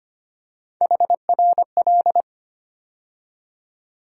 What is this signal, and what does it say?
Morse code, heard as a keyed tone, about 25 wpm, 700 Hz. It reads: HRL
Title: USA Call Sign Postfixes - 3 Letters 25wpm